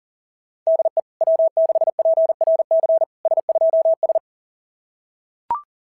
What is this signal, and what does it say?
Morse code, heard as a keyed tone, about 40 wpm, 660 Hz. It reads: DE W6PRC S2S
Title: SOTA QSO Practice 40wpm